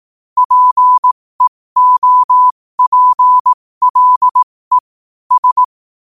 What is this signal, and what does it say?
Morse code, heard as a keyed tone, about 18 wpm, spaced as for 15 wpm, 1000 Hz. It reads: PEOPLE S
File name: 15wpm